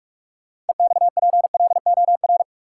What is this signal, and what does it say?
Morse code, heard as a keyed tone, about 41 wpm, 700 Hz. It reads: EXPLOR
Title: Sentences from Top 1100 Words - New Word Drill Alt - 45wpm